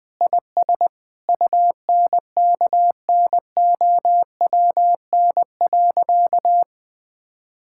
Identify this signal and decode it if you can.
Morse code, heard as a keyed tone, about 20 wpm, 700 Hz. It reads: IS UNKNOWN.
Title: Sentences with 4-Word Phrases - 20wpm